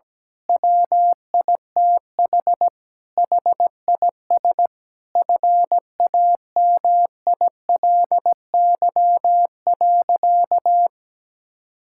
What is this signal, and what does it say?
Morse code, heard as a keyed tone, about 17 wpm, 700 Hz. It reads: WITH HIS FAMILY.
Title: Sentences from Top 1000 Words - Review Part 6 - 17wpm